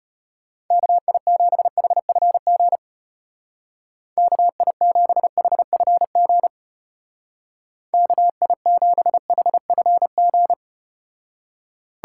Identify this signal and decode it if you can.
Morse code, heard as a keyed tone, about 34 wpm, 700 Hz. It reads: KI7HFG KI7HFG KI7HFG